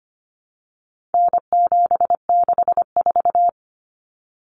Morse code, heard as a keyed tone, about 25 wpm, 700 Hz. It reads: N764
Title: Common Cut Numbers - 4-Digit 25wpm